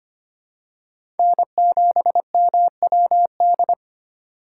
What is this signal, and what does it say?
Morse code, heard as a keyed tone, about 25 wpm, 700 Hz. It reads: N7MWD
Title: USA Call Signs - 6x Character Spacing - 25wpm